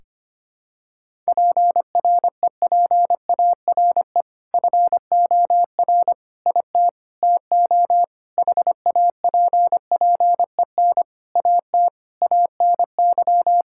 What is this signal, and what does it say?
Morse code, heard as a keyed tone, about 25 wpm, 700 Hz. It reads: PREPARE FOR IT TO HAPPEN AT ANY
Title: Sentences from Top 1000 Words - Review Part 6 - 25wpm